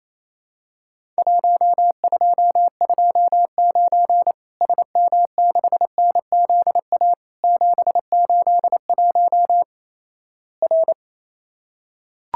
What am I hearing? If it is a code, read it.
Morse code, heard as a keyed tone, about 28 wpm, 700 Hz. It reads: 1229 SM6NZA 781 R
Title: Worked All Europe DX Contest - QTC Exchange - 28wpm